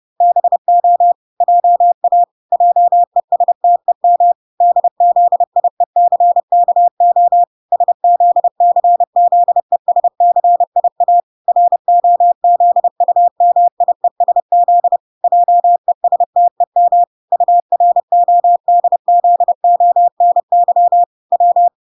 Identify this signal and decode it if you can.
Morse code, heard as a keyed tone, about 30 wpm, 700 Hz. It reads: BO JA JESTEM DZIECKO SZCZESCIA ROZUMIESZ JESTEM URODZONY W